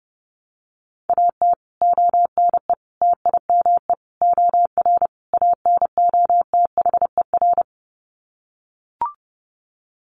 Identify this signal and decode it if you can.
Morse code, heard as a keyed tone, about 30 wpm, 700 Hz. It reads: AT ONE TIME OR ANOTHER
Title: Sets of 5 Words Encore 30wpm